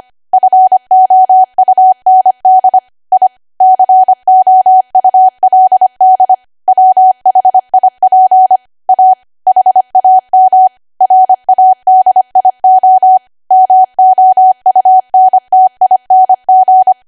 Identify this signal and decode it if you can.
Morse code, heard as a keyed tone, about 25 wpm, 750 Hz. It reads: FOUND I COULD WHIP A HAM RADIO MOUNTING
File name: Archive / 25 WPM / 25WPM